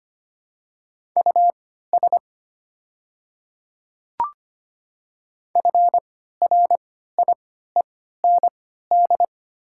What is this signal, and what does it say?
Morse code, heard as a keyed tone, about 25 wpm, spaced as for 14 wpm, 700 Hz. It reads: US FRIEND
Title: 3x Character Spacing - Top 300 Words - 25wpm